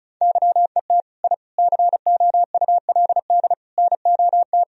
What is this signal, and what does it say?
Morse code, heard as a keyed tone, about 36 wpm, 700 Hz. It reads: YET I COULD NOT
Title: Sentences from Top 1000 Words - Review Part 1 - 35wpm